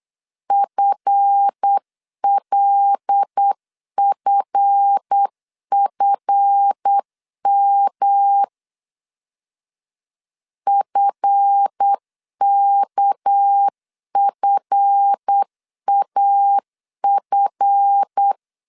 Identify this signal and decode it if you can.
Morse code, heard as a keyed tone, about 8 wpm, 800 Hz. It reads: FLFFM FKFAF
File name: LEZIONI / 7 GF9Z / 7 GF 8wpm 3108